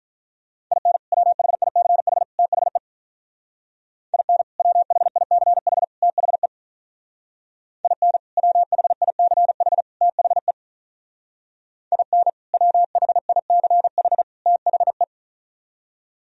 Morse code, heard as a keyed tone, about 46 wpm, 700 Hz. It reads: IN WHICH THE IN WHICH THE IN WHICH THE IN WHICH THE